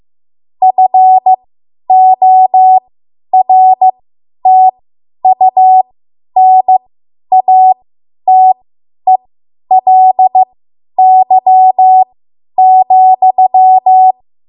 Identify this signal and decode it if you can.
Morse code, heard as a keyed tone, about 15 wpm, spaced as for 10 wpm, 750 Hz. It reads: FORTUNATELY,
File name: Archive / 10 WPM / 10WPM